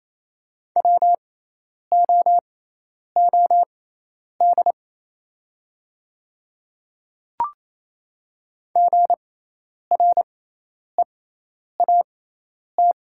Morse code, heard as a keyed tone, about 28 wpm, spaced as for 10 wpm, 700 Hz. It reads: WOOD GREAT
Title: 6x Character Spacing - Top 500 Words - 28wpm